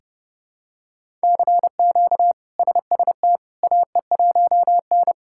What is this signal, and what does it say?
Morse code, heard as a keyed tone, about 30 wpm, 700 Hz. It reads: CQ SST AE1N
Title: K1USN SST QSO Practice - 30wpm